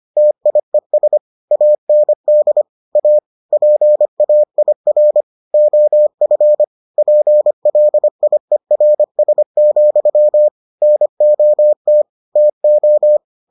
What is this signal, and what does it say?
Morse code, heard as a keyed tone, about 25 wpm, 600 Hz. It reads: TIES AND A PAIR OF PLIERS, NOT TO